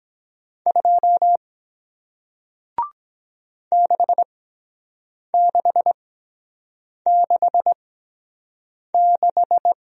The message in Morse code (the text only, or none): 2 6666